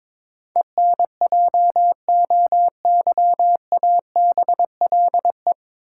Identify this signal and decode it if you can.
Morse code, heard as a keyed tone, about 22 wpm, 700 Hz. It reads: ENJOYABLE
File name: 9 Letter Words - First 3 Letters Given - 22wpm